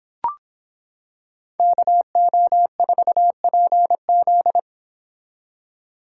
Morse code, heard as a keyed tone, about 25 wpm, 700 Hz. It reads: KO4PZ